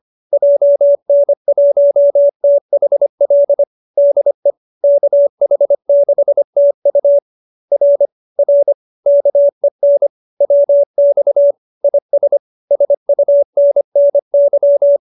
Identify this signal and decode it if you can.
Morse code, heard as a keyed tone, about 25 wpm, 570 Hz. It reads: JN1THL DE KH6TU R R KEN WX IS SUNNY